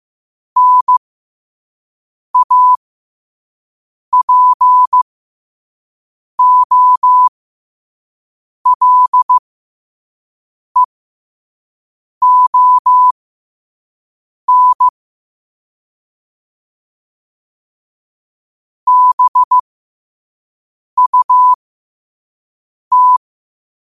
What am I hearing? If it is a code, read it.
Morse code, heard as a keyed tone, about 15 wpm, spaced as for 5 wpm, 1000 Hz. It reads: NAPOLEON BUT
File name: qotdfiles / 5wpm